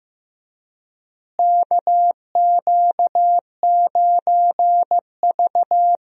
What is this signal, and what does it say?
Morse code, heard as a keyed tone, about 15 wpm, 700 Hz. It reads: KQ9V
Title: USA Call Signs - 4 Characters - Repeated 3x - 15wpm